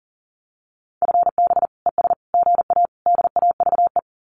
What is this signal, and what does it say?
Morse code, heard as a keyed tone, about 40 wpm, 700 Hz. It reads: FB ES GA DAVE
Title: Common QSO Phrases Encore 40wpm